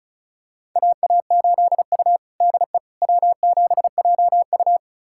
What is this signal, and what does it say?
Morse code, heard as a keyed tone, about 34 wpm, 700 Hz. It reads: AA8U DE W7JU
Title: USA Callsign DE Callsign 35wpm